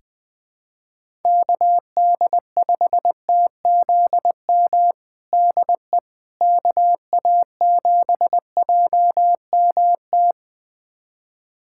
Morse code, heard as a keyed tone, about 20 wpm, 700 Hz. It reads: KD5TZM DE KA7JMT
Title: USA Callsign DE Callsign 20wpm